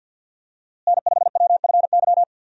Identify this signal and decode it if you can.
Morse code, heard as a keyed tone, about 51 wpm, 700 Hz. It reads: N5WFY
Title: USA Call Signs - 5 Characters - Repeated 3x - 50wpm